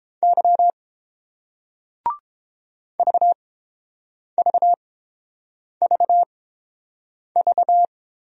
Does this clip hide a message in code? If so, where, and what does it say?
Yes: Morse code from 0.2 to 7.8 s, heard as a keyed tone, about 29 wpm, spaced as for 7 wpm, 690 Hz. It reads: Y VVVV